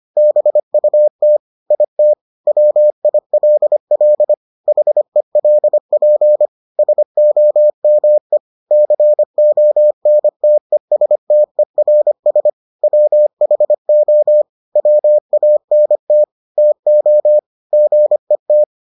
Morse code, heard as a keyed tone, about 25 wpm, 600 Hz. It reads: BUT IT WILL HELP SOME CONTESTERS WHO WANT TO GET